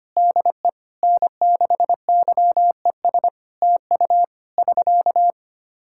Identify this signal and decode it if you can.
Morse code, heard as a keyed tone, about 25 wpm, 700 Hz. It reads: DE N6YES TU <SK>
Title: K1USN SST QSO Practice 25wpm